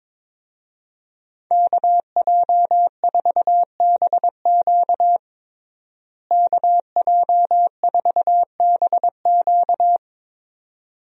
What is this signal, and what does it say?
Morse code, heard as a keyed tone, about 21 wpm, 700 Hz. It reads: KJ4BQ KJ4BQ
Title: USA Call Signs - 5 Characters - Repeated 2x - 22wpm